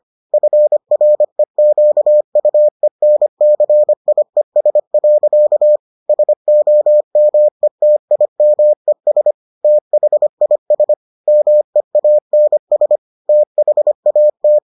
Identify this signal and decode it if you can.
Morse code, heard as a keyed tone, about 25 wpm, 600 Hz. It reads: FREQUENCIES. SOMETIMES THIS MEANS THAT